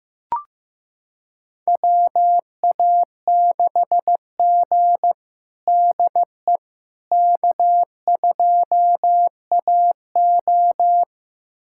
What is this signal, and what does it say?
Morse code, heard as a keyed tone, about 15 wpm, 700 Hz. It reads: WA6G DE K2AO
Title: USA Callsign DE Callsign - 4 Characters - 15wpm